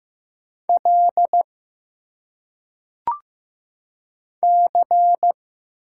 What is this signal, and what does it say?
Morse code, heard as a keyed tone, about 15 wpm, 700 Hz. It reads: L C
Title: Single Letters - Phonetic Pronunciation 15wpm